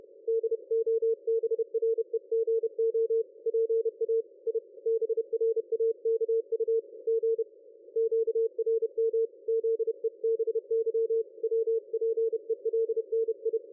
Morse code, heard as a keyed tone, about 30 wpm, 450 Hz. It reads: DOBREGO PA I BRAKU G QRM ZEBY WPELNI
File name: dlazaawansowanych / nagraniazpasma / ExpertSDR2 line out 23.01.2020 14.22.53